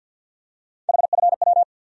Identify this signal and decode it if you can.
Morse code, heard as a keyed tone, about 50 wpm, 700 Hz. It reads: HFW
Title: USA Call Sign Postfixes - 3 Letters 50wpm